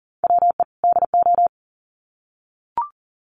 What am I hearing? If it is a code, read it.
Morse code, heard as a keyed tone, about 40 wpm, 700 Hz. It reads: WE DO
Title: Sets of 2 Words 40wpm